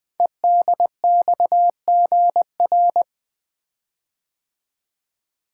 Morse code, heard as a keyed tone, about 20 wpm, 700 Hz. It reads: EDXGR
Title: Code Groups - 5 Letters - 20wpm